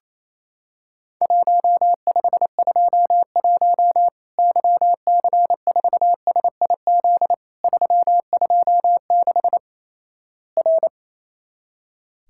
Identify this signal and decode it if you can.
Morse code, heard as a keyed tone, about 28 wpm, 700 Hz. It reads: 1521 YC4SIZ 326 R
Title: Worked All Europe DX Contest - QTC Exchange - 28wpm